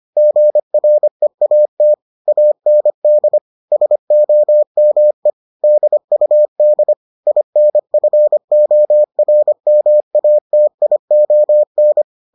Morse code, heard as a keyed tone, about 25 wpm, 600 Hz. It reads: GREAT AND SOME DUD INFORMATION